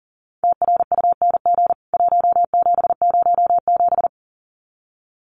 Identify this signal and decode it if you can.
Morse code, heard as a keyed tone, about 40 wpm, 700 Hz. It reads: TRUNG 1707
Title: CWT Exchange 40wpm